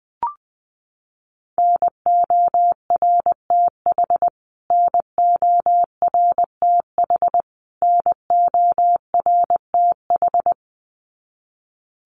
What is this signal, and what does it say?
Morse code, heard as a keyed tone, about 20 wpm, 700 Hz. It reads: NORTH NORTH NORTH
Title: Top 500 Words Repeated 20wpm